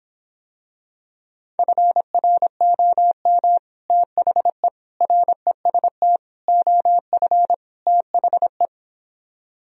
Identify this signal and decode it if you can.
Morse code, heard as a keyed tone, about 26 wpm, 700 Hz. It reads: FROM THE REST OF THE